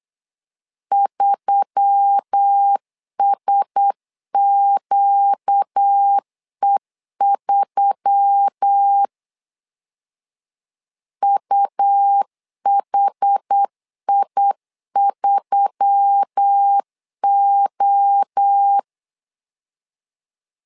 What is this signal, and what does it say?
Morse code, heard as a keyed tone, about 8 wpm, 800 Hz. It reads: 3SQE3 UHI3O
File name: LEZIONI / 4 Q1S3 8wpm 3037